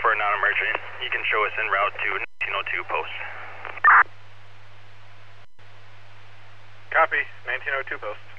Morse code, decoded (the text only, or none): none